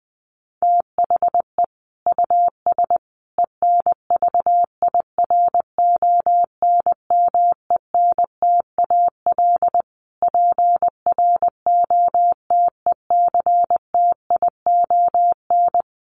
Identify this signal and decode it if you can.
Morse code, heard as a keyed tone, about 19 wpm, 700 Hz. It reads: THE US ENVIRONMENTAL PROTECTION
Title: Sets of 5 Words Encore2 20wpm